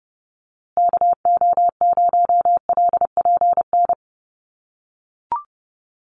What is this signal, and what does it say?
Morse code, heard as a keyed tone, about 30 wpm, 700 Hz. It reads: KO0LPN
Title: USA Call Signs Repeated 3x 30wpm